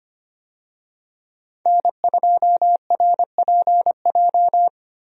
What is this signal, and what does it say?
Morse code, heard as a keyed tone, about 25 wpm, 700 Hz. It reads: N2RPJ
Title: USA Call Signs - 5 Characters - 25wpm